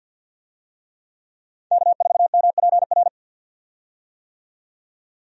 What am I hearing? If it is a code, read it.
Morse code, heard as a keyed tone, about 50 wpm, 700 Hz. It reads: K4MPR
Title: USA Call Signs - 5 Characters - 50wpm